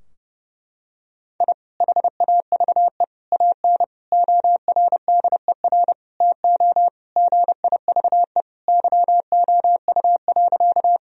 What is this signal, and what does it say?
Morse code, heard as a keyed tone, about 30 wpm, 700 Hz. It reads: I HAVE AN ORDER TO GIVE YOU.
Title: Sentences from Top 1000 Words - Review Part 2 - 30wpm